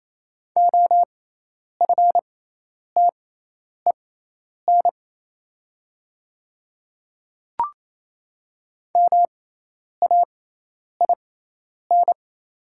Morse code, heard as a keyed tone, about 28 wpm, spaced as for 10 wpm, 700 Hz. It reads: OFTEN MAIN
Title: 6x Character Spacing - Top 300 Words - 28wpm